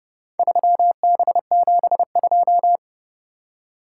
Morse code, heard as a keyed tone, about 29 wpm, 700 Hz. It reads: 3B72